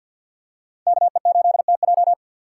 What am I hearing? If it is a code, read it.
Morse code, heard as a keyed tone, about 49 wpm, 700 Hz. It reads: KE8TJ
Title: USA Call Signs - 5 Characters - 2x Character Spacing - 50wpm